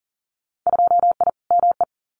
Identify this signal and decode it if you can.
Morse code, heard as a keyed tone, about 41 wpm, 700 Hz. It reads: 2I ME
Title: Winter Field Day Exchanges Encore 40wpm